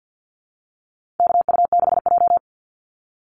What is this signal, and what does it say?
Morse code, heard as a keyed tone, about 50 wpm, 700 Hz. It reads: KU6J
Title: NA SOTA Chaser Callsigns 50wpm